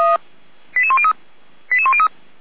Five-tone selective call (ZVEI), heard as quick runs of five tones, tones of about 70 ms. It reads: 88103 88103